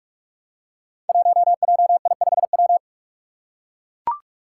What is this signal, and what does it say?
Morse code, heard as a keyed tone, about 44 wpm, 700 Hz. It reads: 1JI5W
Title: Five Challenging Letter-Number Combinations 45wpm